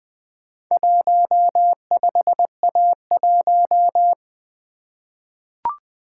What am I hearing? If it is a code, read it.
Morse code, heard as a keyed tone, about 20 wpm, 700 Hz. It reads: 15A1
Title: Four Letter-Number Combinations - 20wpm